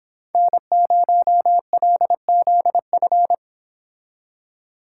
Morse code, heard as a keyed tone, about 26 wpm, 700 Hz. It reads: N0LZF
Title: Speed Racing - USA 1x3 Call Signs 22wpm